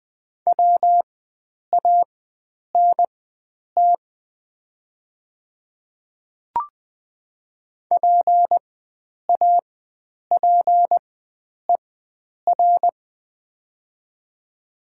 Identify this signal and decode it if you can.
Morse code, heard as a keyed tone, about 20 wpm, spaced as for 9 wpm, 700 Hz. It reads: WANT PAPER
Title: 4x Character Spacing - Top 300 Words - 20wpm